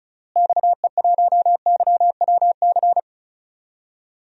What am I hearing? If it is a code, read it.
Morse code, heard as a keyed tone, about 35 wpm, 700 Hz. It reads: XE1YWC
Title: DXCC-100 Call Signs - 6 Characters - Repeated 3x - 35wpm